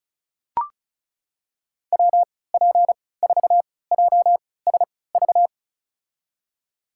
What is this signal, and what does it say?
Morse code, heard as a keyed tone, about 34 wpm, spaced as for 20 wpm, 700 Hz. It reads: WP4JSV
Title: USA Call Signs - 6 Characters - 3x Character Spacing - 35wpm